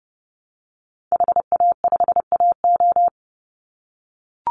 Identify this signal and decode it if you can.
Morse code, heard as a keyed tone, about 30 wpm, 700 Hz. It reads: HA5AO
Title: Call Signs - 30wpm